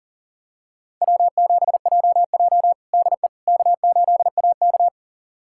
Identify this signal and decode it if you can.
Morse code, heard as a keyed tone, about 40 wpm, 700 Hz. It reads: W7JJ DE K8AK
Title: USA Callsign DE Callsign - 4 Characters - 40wpm